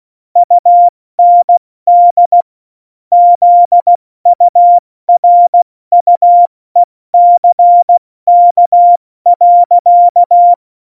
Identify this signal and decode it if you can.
Morse code, heard as a keyed tone, about 16 wpm, 700 Hz. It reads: UND ZURUECK.
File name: Morsekurs / Witze / Witz 19. Januar 2021, 70 F80